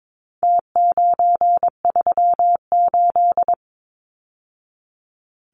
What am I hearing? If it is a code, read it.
Morse code, heard as a keyed tone, about 22 wpm, 700 Hz. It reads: T938
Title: Numbers - Contest Style - 4-Digit 22wpm